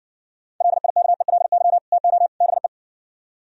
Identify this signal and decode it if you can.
Morse code, heard as a keyed tone, about 69 wpm, 700 Hz. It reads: LIKELY TO BE